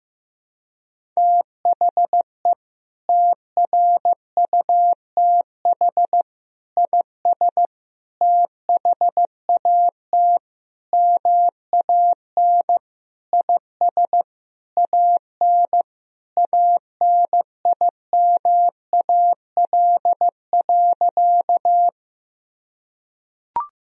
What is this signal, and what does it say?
Morse code, heard as a keyed tone, about 15 wpm, 700 Hz. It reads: THE TRUTH IS THAT MAN IS AN ANIMAL.
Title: Sentences from Top 1000 Words - Review Part 1 - 15wpm